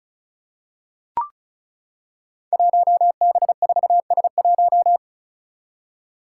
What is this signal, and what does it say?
Morse code, heard as a keyed tone, about 34 wpm, 700 Hz. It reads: 1B4S1
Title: Five Challenging Letter-Number Combinations 35wpm